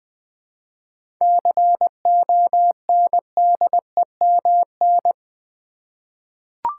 Morse code, heard as a keyed tone, about 20 wpm, 690 Hz. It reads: CONDEMN
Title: 7 Letter Words - First 3 Letters Given - 20wpm